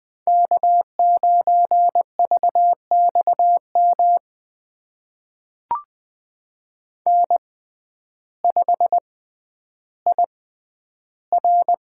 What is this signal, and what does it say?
Morse code, heard as a keyed tone, about 20 wpm, spaced as for 11 wpm, 700 Hz. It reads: K9VXM N5IR
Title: USA Call Signs - 5 Characters - 6x Character Spacing - 20wpm